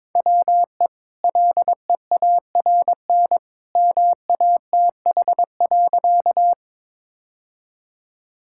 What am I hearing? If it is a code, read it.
Morse code, heard as a keyed tone, about 22 wpm, 700 Hz. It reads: WE LEARN MATH.
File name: practice / 22wpm / Easy Sentences with Hints - 22wpm